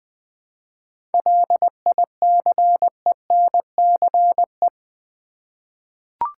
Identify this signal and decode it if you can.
Morse code, heard as a keyed tone, about 20 wpm, 700 Hz. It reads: LICENCE
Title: 7 Letter Words - First 3 Letters Given - 20wpm